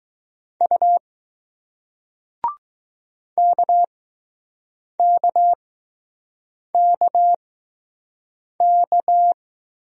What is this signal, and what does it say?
Morse code, heard as a keyed tone, about 20 wpm, spaced as for 6 wpm, 700 Hz. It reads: U KKKK